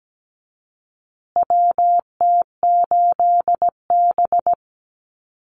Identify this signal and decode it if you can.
Morse code, heard as a keyed tone, about 17 wpm, 700 Hz. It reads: WT8B
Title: USA Call Signs Repeated 3x 17wpm